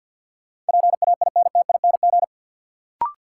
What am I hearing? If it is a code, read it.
Morse code, heard as a keyed tone, about 51 wpm, 700 Hz. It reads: PAINTING
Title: Sentences from Top 1000 Words - Review Part 6 - New Word Drill - 50wpm